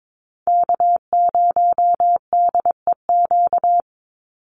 Morse code, heard as a keyed tone, about 22 wpm, 700 Hz. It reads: K0DEQ
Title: Call Signs Repeated 2x - 22wpm